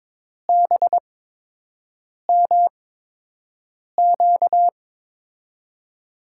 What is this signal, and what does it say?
Morse code, heard as a keyed tone, about 22 wpm, spaced as for 6 wpm, 700 Hz. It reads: BMQ
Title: USA Call Signs - 6 Characters - 8x Character Spacing - 22wpm